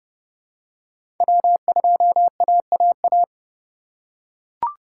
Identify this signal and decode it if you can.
Morse code, heard as a keyed tone, about 29 wpm, 700 Hz. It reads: W2AAA
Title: USA Call Signs - 5 Characters - 3x Character Spacing - 30wpm